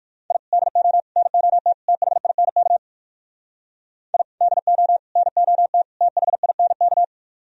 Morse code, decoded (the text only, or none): I DO NOT THINK I DO NOT THINK